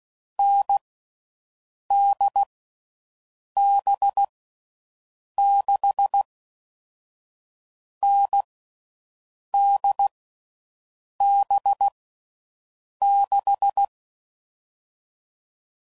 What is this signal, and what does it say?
Morse code, heard as a keyed tone, about 16 wpm, 800 Hz. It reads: N D B 6 N D B 6
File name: Mixed / Short Runs / NDB NDB6